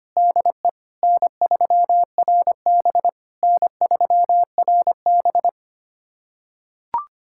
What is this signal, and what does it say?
Morse code, heard as a keyed tone, about 25 wpm, 700 Hz. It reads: DE N3RB N3RB
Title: SOTA QSO Practice 25wpm